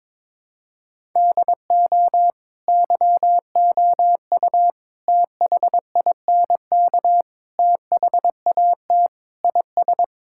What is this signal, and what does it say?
Morse code, heard as a keyed tone, about 22 wpm, 700 Hz. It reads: DO YOU THINK THAT IS
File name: practice / 22wpm / Sentences with 4-Word Phrases - 22wpm